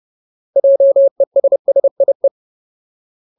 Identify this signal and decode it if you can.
Morse code, heard as a keyed tone, about 30 wpm, 550 Hz. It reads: JESSIE
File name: Ultimate Word List / 30WPM Ultimate